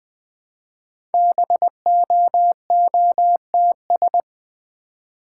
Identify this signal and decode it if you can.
Morse code, heard as a keyed tone, about 20 wpm, 700 Hz. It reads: BOOTS
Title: Easy Words with Hints - 20wpm